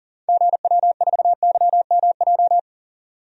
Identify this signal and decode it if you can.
Morse code, heard as a keyed tone, about 38 wpm, 700 Hz. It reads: GW4YMJ
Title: DXCC-100 Call Signs - 6 Characters - 40wpm